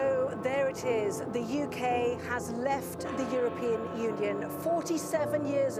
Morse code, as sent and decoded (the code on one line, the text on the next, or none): none